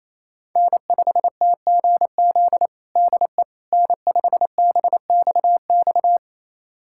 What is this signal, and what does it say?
Morse code, heard as a keyed tone, about 28 wpm, 700 Hz. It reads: N5TGZ DE N5BXX
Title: USA Callsign DE Callsign - 5 Characters - 28wpm